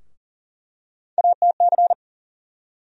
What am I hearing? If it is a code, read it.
Morse code, heard as a keyed tone, about 40 wpm, 700 Hz. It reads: ATC
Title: USA Call Sign Postfixes - 3 Letters 40wpm